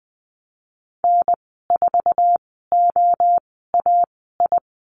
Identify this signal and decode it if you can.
Morse code, heard as a keyed tone, about 20 wpm, spaced as for 14 wpm, 700 Hz. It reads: N4OAI
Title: 048 Lesson 9g - Number 4 - Call Signs 20wpm